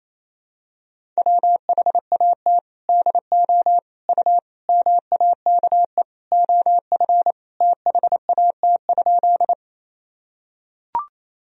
Morse code, heard as a keyed tone, about 28 wpm, 700 Hz. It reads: WHAT DO U MAKE OF THAT?
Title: Sentences from Top 100 Words and Abbreviations 28wpm